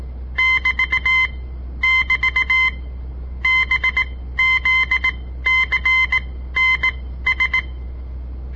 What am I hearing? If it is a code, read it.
Morse code, heard as a keyed tone, about 17 wpm, spaced as for 13 wpm, 2050 Hz. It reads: <BT><BT> BZCNS
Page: On the air Morse code practice from the Southern Montana Amatuer Radio Association; all code, no read back